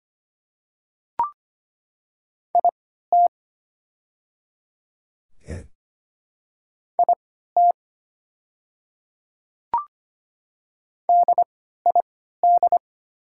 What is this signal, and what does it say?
Morse code, heard as a keyed tone, about 25 wpm, spaced as for 14 wpm, 700 Hz. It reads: IT IT DID
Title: 3x Character Spacing - Top 100 Words - 25wpm